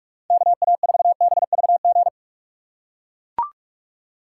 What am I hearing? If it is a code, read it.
Morse code, heard as a keyed tone, about 42 wpm, 700 Hz. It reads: KA4BVG
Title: K1USN SST QSO Practice - 45wpm